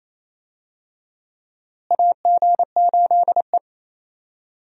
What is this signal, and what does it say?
Morse code, heard as a keyed tone, about 28 wpm, 700 Hz. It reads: AG8E
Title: USA Call Signs - 4 Characters - Repeated 2x - 28wpm